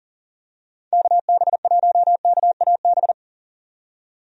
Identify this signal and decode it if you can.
Morse code, heard as a keyed tone, about 40 wpm, 700 Hz. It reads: KB1KAB
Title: USA Call Signs - 6 Characters - 40wpm